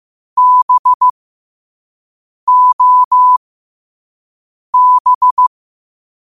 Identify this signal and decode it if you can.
Morse code, heard as a keyed tone, about 15 wpm, spaced as for 5 wpm, 1000 Hz. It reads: BOB
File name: 5wpm